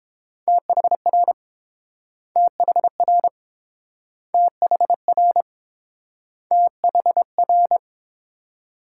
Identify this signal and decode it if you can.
Morse code, heard as a keyed tone, about 30 wpm, 700 Hz. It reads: THR THR THR THR